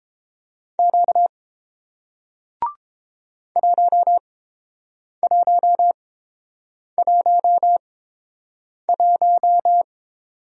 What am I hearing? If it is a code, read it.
Morse code, heard as a keyed tone, about 30 wpm, spaced as for 7 wpm, 700 Hz. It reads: Q 1111